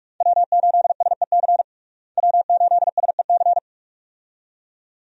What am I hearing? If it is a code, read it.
Morse code, heard as a keyed tone, about 46 wpm, 700 Hz. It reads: W8SEC W8SEC
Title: USA Call Signs - 5 Characters - Repeated 2x - 45wpm